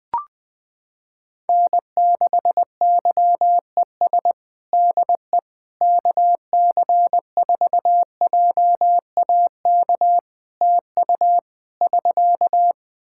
Morse code, heard as a keyed tone, about 20 wpm, 700 Hz. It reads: N6YES DE KC4JAK TU <SK>
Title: K1USN SST QSO Practice 20wpm